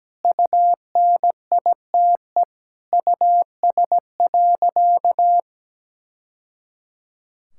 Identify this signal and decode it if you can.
Morse code, heard as a keyed tone, about 17 wpm, 700 Hz. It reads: UNITE US.
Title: Sentences from Top 1000 Words - Review Part 5 - 17wpm